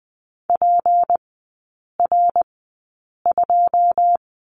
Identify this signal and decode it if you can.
Morse code, heard as a keyed tone, about 20 wpm, spaced as for 8 wpm, 700 Hz. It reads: PR2
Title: 218 Lesson 35e - Number 0 - Three Characters 20wpm